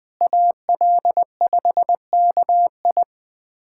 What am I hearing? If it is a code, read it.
Morse code, heard as a keyed tone, about 20 wpm, 700 Hz. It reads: AL5KI
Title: USA Callsign DE Callsign - 5 Characters - 20wpm